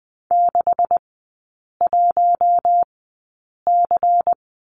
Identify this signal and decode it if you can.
Morse code, heard as a keyed tone, about 20 wpm, spaced as for 8 wpm, 700 Hz. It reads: 61C
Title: 152 Lesson 25e - Letter p - Three Characters 20wpm